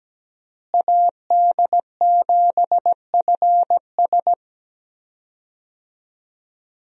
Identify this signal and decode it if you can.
Morse code, heard as a keyed tone, about 17 wpm, 700 Hz. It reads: AD7FS
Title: USA Call Signs - 5 Characters - Repeated 2x - 17wpm